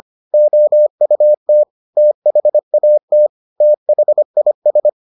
Morse code, heard as a keyed tone, about 25 wpm, 600 Hz. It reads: OUT THAT THIS